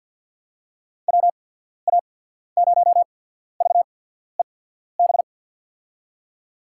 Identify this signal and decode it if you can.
Morse code, heard as a keyed tone, about 47 wpm, spaced as for 14 wpm, 700 Hz. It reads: WA0VEB
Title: USA Call Signs - 6 Characters - 8x Character Spacing - 50wpm